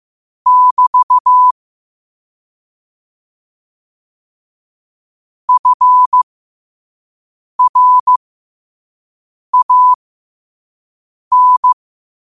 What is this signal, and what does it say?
Morse code, heard as a keyed tone, about 15 wpm, spaced as for 5 wpm, 1000 Hz. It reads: <BT> FRAN
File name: qotdfiles / 5wpm